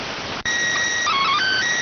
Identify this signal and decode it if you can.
SSTV (slow-scan television), PD 290.